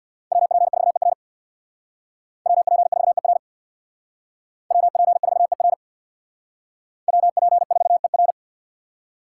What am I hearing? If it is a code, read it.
Morse code, heard as a keyed tone, about 62 wpm, 700 Hz. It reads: WP4ER WP4ER WP4ER WP4ER
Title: Speed Racing - USA 2x2 Call Signs 50wpm